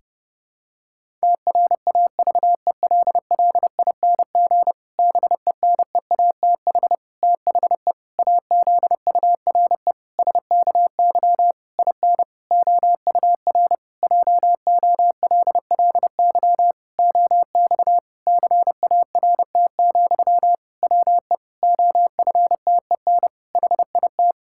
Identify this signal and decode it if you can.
Morse code, heard as a keyed tone, about 30 wpm, 700 Hz. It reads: TRAVELLING BENEATH THE AZURE SKY IN OUR JOLLY OX CART, WE OFTEN HIT
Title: Pangram Challenge v2 - 30wpm